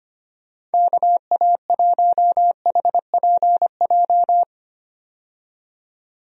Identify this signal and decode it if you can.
Morse code, heard as a keyed tone, about 25 wpm, 700 Hz. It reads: KA1HPJ
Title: USA Call Signs - 6 Characters - 25wpm